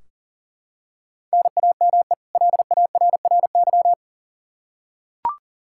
Morse code, heard as a keyed tone, about 40 wpm, 700 Hz. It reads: NAME LARRY
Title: Practice QSO - Exchange 1 - Beginning - 40wpm